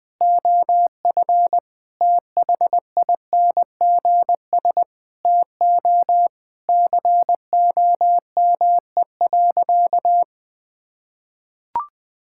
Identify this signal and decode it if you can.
Morse code, heard as a keyed tone, about 21 wpm, 700 Hz. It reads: OF THINGS TO COME.
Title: Sentences with 6-Word Phrases - 20wpm